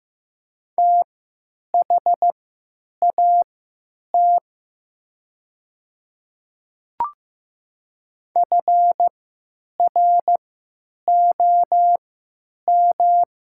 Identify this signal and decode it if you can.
Morse code, heard as a keyed tone, about 15 wpm, spaced as for 9 wpm, 700 Hz. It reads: THAT FROM
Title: 3x Character Spacing - Top 200 Words - 15wpm